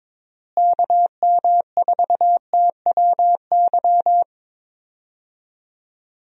Morse code, heard as a keyed tone, about 22 wpm, 700 Hz. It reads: KM4TWY
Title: USA Call Signs - 6 Characters - Repeated 2x - 22wpm